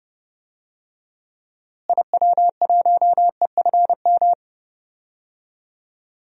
Morse code, heard as a keyed tone, about 29 wpm, 700 Hz. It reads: IW1EFM
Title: DXCC-100 Call Signs - 6 Characters - Repeated 2x - 30wpm